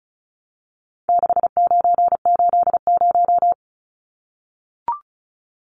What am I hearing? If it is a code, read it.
Morse code, heard as a keyed tone, about 34 wpm, 700 Hz. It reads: 6980
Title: Numbers - 4-Digit 35wpm